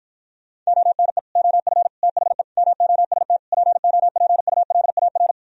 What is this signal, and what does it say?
Morse code, heard as a keyed tone, about 44 wpm, 700 Hz. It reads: ONE OF THE MOST POPULAR
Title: Speed-Racing - Sets of 5 Words 35wpm